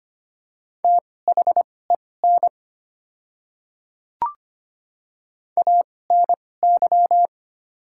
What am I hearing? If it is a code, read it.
Morse code, heard as a keyed tone, about 25 wpm, spaced as for 18 wpm, 690 Hz. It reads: THEN ANY
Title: 2x Character Spacing - Top 200 Words - 25wpm